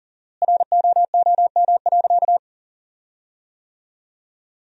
Morse code, heard as a keyed tone, about 40 wpm, 700 Hz. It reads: ROOM.
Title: 4x Word-Spacing - Sentences from Top 500 Words - 40wpm